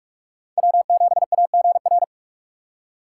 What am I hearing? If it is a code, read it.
Morse code, heard as a keyed tone, about 46 wpm, 700 Hz. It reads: W7AGR
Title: USA Call Signs - 3x Character Spacing - 45wpm